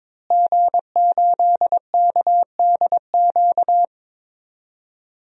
Morse code, heard as a keyed tone, about 21 wpm, 700 Hz. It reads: G8KDQ
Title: DXCC-100 Call Signs - 5 Characters - Repeated 2x - 22wpm